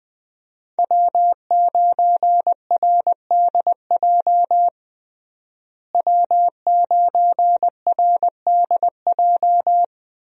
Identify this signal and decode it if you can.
Morse code, heard as a keyed tone, about 21 wpm, 700 Hz. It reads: W9RDJ W9RDJ
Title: USA Call Signs - 5 Characters - Repeated 2x - 20wpm